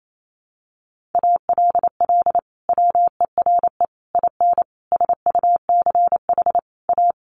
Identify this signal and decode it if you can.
Morse code, heard as a keyed tone, about 28 wpm, 700 Hz. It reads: ALL WERE IN SUCH A
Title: Sentences from Top 1100 Words 28wpm